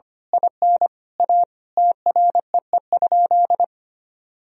Morse code, heard as a keyed tone, about 25 wpm, 700 Hz. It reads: IN A TREE?
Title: Sentences from Top 1000 Words - Review Part 1 - 25wpm